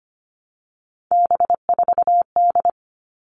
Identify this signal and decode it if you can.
Morse code, heard as a keyed tone, about 25 wpm, 700 Hz. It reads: B4D
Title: All Cut Numbers - 3-Digit 25wpm